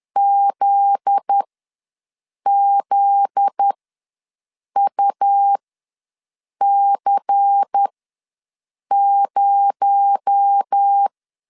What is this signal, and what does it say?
Morse code, heard as a keyed tone, about 10 wpm, spaced as for 6 wpm, 800 Hz. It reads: ZZUC0
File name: LEZIONI / 7 GF9Z / 7 GF9Z ws 10wpm 3919